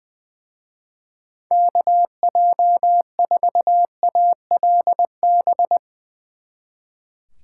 Morse code, heard as a keyed tone, about 20 wpm, 700 Hz. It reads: KJ4ALB
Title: Speed Racing - USA 2x3 Call Signs 20wpm